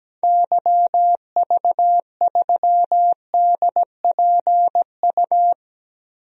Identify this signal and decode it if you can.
Morse code, heard as a keyed tone, about 17 wpm, 700 Hz. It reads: YV3DPU
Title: DXCC-100 Call Signs - 6 Characters - 17wpm